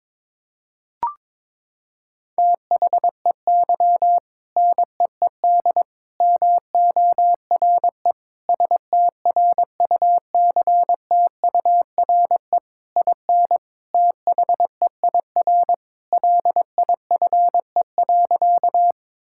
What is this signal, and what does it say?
Morse code, heard as a keyed tone, about 22 wpm, 700 Hz. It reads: THEY NEED MORE STRUCTURE IN THEIR LIFE.
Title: Sentences from Top 1000 Words - Review Part 6 - 22wpm